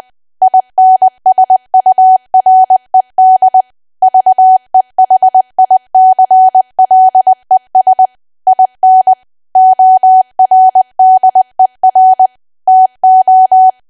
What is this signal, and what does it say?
Morse code, heard as a keyed tone, about 20 wpm, 750 Hz. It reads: INSURED VEHICLES IN ORDER TO
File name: cwa / W1AW-Level3 / 20WPM